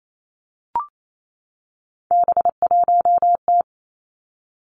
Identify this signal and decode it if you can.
Morse code, heard as a keyed tone, about 28 wpm, 700 Hz. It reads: B1T
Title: Three Challenging Letter-Number Combinations 28wpm